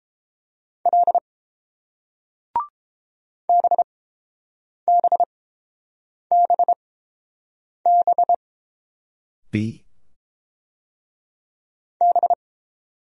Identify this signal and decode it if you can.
Morse code, heard as a keyed tone, about 30 wpm, 700 Hz. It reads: L B B B B B